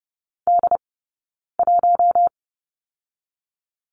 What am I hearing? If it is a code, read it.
Morse code, heard as a keyed tone, about 30 wpm, 700 Hz. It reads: D 1
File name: course / 30:10wpm / 099 Lesson 17d - Letter u - Two Characters 30wpm